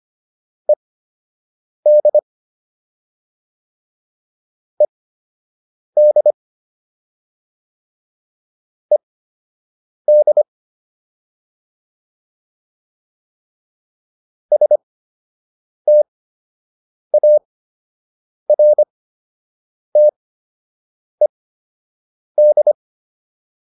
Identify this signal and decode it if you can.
Morse code, heard as a keyed tone, about 25 wpm, spaced as for 7 wpm, 600 Hz. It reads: ED ED ED STARTED